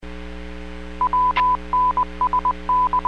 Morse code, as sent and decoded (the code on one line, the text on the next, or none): .-- -. ... -.
WNSN